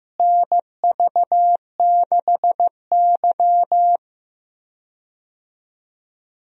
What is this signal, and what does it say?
Morse code, heard as a keyed tone, about 15 wpm, 700 Hz. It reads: NV6Y
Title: USA Call Signs - 4 Characters - Repeated 3x - 15wpm